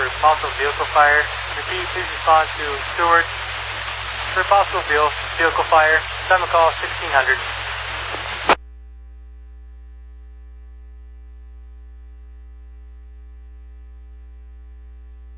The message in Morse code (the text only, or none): none